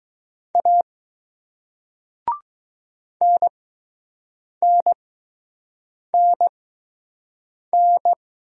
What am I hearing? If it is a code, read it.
Morse code, heard as a keyed tone, about 20 wpm, spaced as for 6 wpm, 700 Hz. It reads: A NNNN